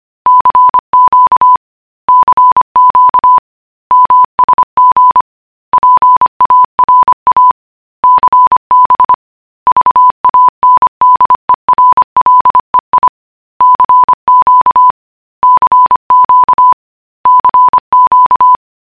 Morse code, heard as a keyed tone, about 25 wpm, 1000 Hz. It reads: CQ CQ MSG PARA CB VANDERLEI CQ CQ CQ